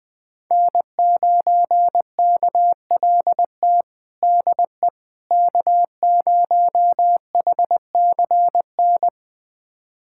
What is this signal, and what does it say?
Morse code, heard as a keyed tone, about 20 wpm, 700 Hz. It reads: N9KLT DE K0HCN
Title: USA Callsign DE Callsign - 5 Characters - 20wpm